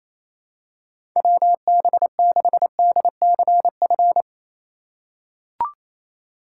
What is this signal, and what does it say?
Morse code, heard as a keyed tone, about 28 wpm, 700 Hz. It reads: WB6DCF
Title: USA Call Signs - 6 Characters - 2x Character Spacing - 28wpm